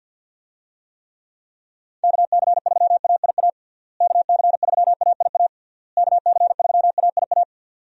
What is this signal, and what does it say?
Morse code, heard as a keyed tone, about 51 wpm, 700 Hz. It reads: KC3AIA KC3AIA KC3AIA
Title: USA Call Signs - 6 Characters - Repeated 3x - 50wpm